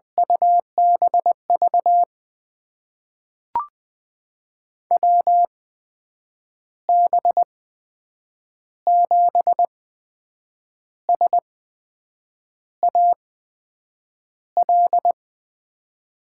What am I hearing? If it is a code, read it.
Morse code, heard as a keyed tone, about 21 wpm, spaced as for 7 wpm, 700 Hz. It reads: UBV WB7SAL
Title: USA Call Signs - 6 Characters - 8x Character Spacing - 20wpm